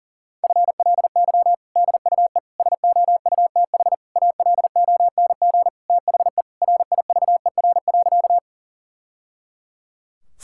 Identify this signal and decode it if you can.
Morse code, heard as a keyed tone, about 40 wpm, 700 Hz. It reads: FLY DUE SOUTH ALONG THE RIVER.
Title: Sentences from Top 1000 Words - Review Part 6 - 40wpm